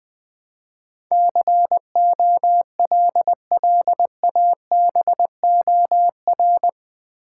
Morse code, heard as a keyed tone, about 20 wpm, 700 Hz. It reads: COLLABOR
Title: Build A Word - 4 Letters At A Time - 11-Letter Words - 20wpm